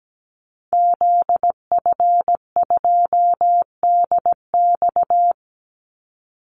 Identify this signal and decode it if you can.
Morse code, heard as a keyed tone, about 17 wpm, 700 Hz. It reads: ZF2DX
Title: Call Signs - 17wpm